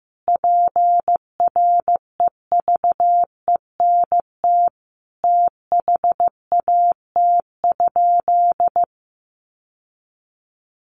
Morse code, heard as a keyed tone, about 15 wpm, 700 Hz. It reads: PREVENT THAT?
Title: Sentences from Top 1100 Words 15wpm